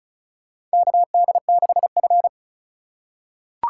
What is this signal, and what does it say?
Morse code, heard as a keyed tone, about 34 wpm, 700 Hz. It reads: KD6F
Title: USA Call Signs - 4 Characters - 3x Character Spacing - 35wpm